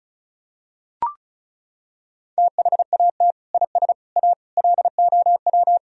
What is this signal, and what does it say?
Morse code, heard as a keyed tone, about 36 wpm, 700 Hz. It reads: THAT IS A LOW
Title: Sentences from Top 1000 Words - Review Part 1 - 35wpm